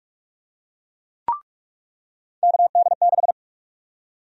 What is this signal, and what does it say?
Morse code, heard as a keyed tone, about 43 wpm, 700 Hz. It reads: KD6
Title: USA Call Sign Prefixes - 45wpm